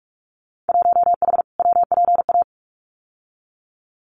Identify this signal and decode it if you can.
Morse code, heard as a keyed tone, about 44 wpm, 700 Hz. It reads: 1H WPA
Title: Winter Field Day Exchanges 45wpm